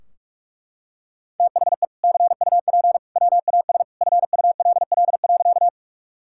Morse code, heard as a keyed tone, about 41 wpm, 700 Hz. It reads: THE CUP WAS FULL.
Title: Sentences from Top 1600 Words 45wpm